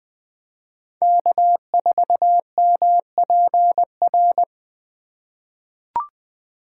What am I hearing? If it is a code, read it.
Morse code, heard as a keyed tone, about 20 wpm, 700 Hz. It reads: K4MPR
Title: USA Call Signs - 5 Characters - 4x Character Spacing - 20wpm